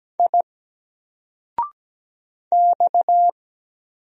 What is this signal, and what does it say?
Morse code, heard as a keyed tone, about 17 wpm, 700 Hz. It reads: I X
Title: Single Character (Letter:Number:Punctuation:Prosign) 17wpm